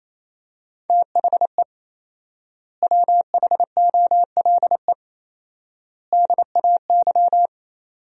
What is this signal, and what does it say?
Morse code, heard as a keyed tone, about 28 wpm, 700 Hz. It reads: THE WHOLE DAY
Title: 4x Word-Spacing - Sentences from Top 500 Words - 28wpm